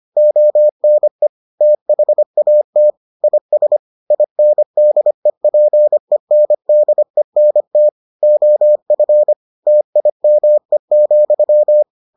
Morse code, heard as a keyed tone, about 25 wpm, 600 Hz. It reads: ONE THAT IS INDEPENDENT OF TIME,